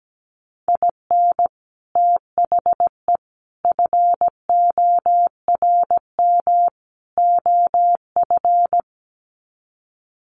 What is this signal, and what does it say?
Morse code, heard as a keyed tone, about 17 wpm, 700 Hz. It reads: IN THE FORM OF
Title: Sets of 4 Words 17wpm